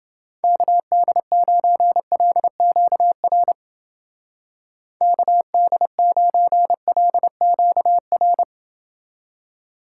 Morse code, heard as a keyed tone, about 29 wpm, 700 Hz. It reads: KD9LQR KD9LQR